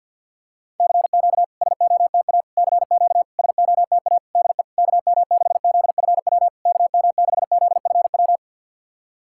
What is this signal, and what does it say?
Morse code, heard as a keyed tone, about 53 wpm, 690 Hz. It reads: CQ SOTA CQ SOTA DE KM6ZFW KM6ZFW
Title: SOTA QSO Practice 50wpm